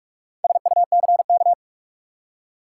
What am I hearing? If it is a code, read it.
Morse code, heard as a keyed tone, about 42 wpm, 700 Hz. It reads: SUCK
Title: Four Letter Words - 45wpm